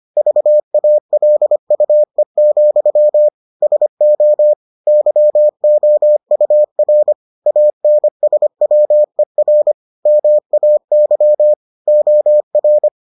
Morse code, heard as a keyed tone, about 25 wpm, 600 Hz. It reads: VALUE, SO YOUR ANSWER MAY OR